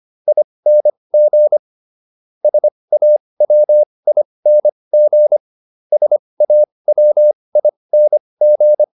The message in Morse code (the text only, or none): ING SAWING SAWING